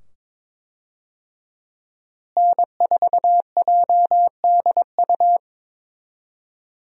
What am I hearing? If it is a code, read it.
Morse code, heard as a keyed tone, about 22 wpm, 700 Hz. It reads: N4JDU
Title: USA Call Signs - 5 Characters - Repeated 2x - 22wpm